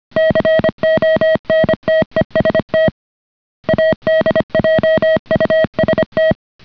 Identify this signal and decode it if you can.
Morse code, heard as a keyed tone, about 25 wpm, 630 Hz. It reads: CONTEST ADJUST